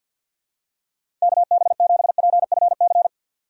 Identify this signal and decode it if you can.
Morse code, heard as a keyed tone, about 50 wpm, 700 Hz. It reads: KB7PFC
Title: USA Call Signs - 6 Characters - 4x Character Spacing - 50wpm